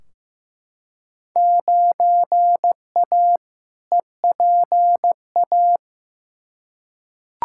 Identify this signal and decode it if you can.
Morse code, heard as a keyed tone, about 15 wpm, 700 Hz. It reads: 9A EPA
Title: Field Day Exchanges Encore - 15wpm